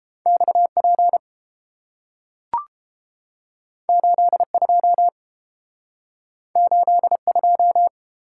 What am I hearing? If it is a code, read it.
Morse code, heard as a keyed tone, about 30 wpm, 700 Hz. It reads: XP 82 82